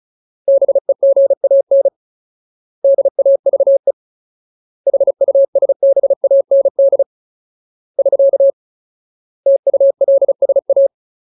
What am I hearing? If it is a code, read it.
Morse code, heard as a keyed tone, about 34 wpm, 550 Hz. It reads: BEGAN DAVE HUSBAND <SK> TULSA
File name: Ultimate Word List / 35WPM Ultimate